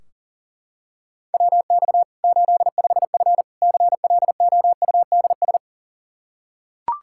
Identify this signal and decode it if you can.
Morse code, heard as a keyed tone, about 40 wpm, 700 Hz. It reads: WX 85F CLOUDS
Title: Practice QSO - Exchange 2 - Info - 40wpm